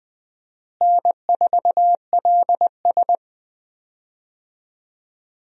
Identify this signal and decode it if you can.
Morse code, heard as a keyed tone, about 21 wpm, 700 Hz. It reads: N4LS
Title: USA Call Signs - 4 Characters - Repeated 3x - 20wpm